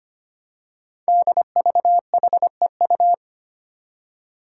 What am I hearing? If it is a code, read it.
Morse code, heard as a keyed tone, about 25 wpm, 700 Hz. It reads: DVHEU
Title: Five Challenging Letter-Number Combinations 25wpm